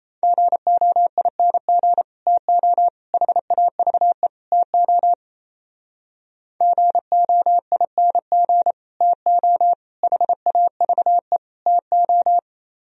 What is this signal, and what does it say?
Morse code, heard as a keyed tone, about 34 wpm, 700 Hz. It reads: GOING TO HAVE TO GOING TO HAVE TO
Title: Speed-Racing - Sets of 4 Words 28wpm